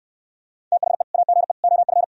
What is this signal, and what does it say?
Morse code, heard as a keyed tone, about 65 wpm, 700 Hz. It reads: THE AGE OF